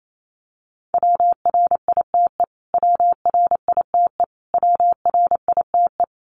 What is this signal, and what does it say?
Morse code, heard as a keyed tone, about 28 wpm, 700 Hz. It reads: WRITE WRITE WRITE
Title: Top 300 Words Repeated 28wpm